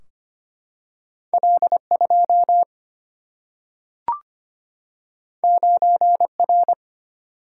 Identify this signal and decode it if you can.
Morse code, heard as a keyed tone, about 25 wpm, 700 Hz. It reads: L 2 9 R
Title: Single-Letter Single-Number 25wpm